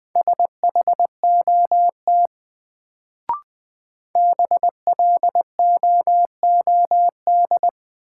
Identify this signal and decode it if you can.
Morse code, heard as a keyed tone, about 20 wpm, 700 Hz. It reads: SHOT BLOOD
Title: Sentences from Top 1100 Words - New Word Drill Alt - 20wpm